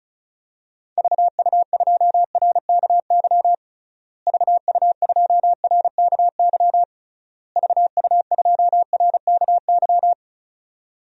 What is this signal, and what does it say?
Morse code, heard as a keyed tone, about 34 wpm, 700 Hz. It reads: VU2RKY VU2RKY VU2RKY
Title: DXCC-100 Call Signs - 6 Characters - Repeated 3x - 35wpm